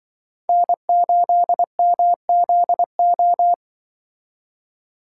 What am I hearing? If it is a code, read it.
Morse code, heard as a keyed tone, about 24 wpm, 700 Hz. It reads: N8MZO
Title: Speed Racing - USA 1x3 Call Signs 20wpm